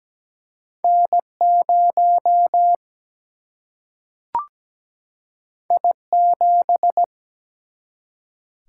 Morse code, heard as a keyed tone, about 17 wpm, 700 Hz. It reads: N0 I7
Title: Single-Letter Single-Number 17wpm